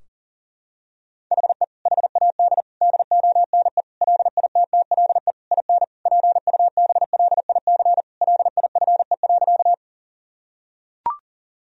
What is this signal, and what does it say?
Morse code, heard as a keyed tone, about 40 wpm, 700 Hz. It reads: HE HAD DONE LITTLE IN PUBLIC LIFE.
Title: Sentences from Top 1000 Words - Review Part 3 - 40wpm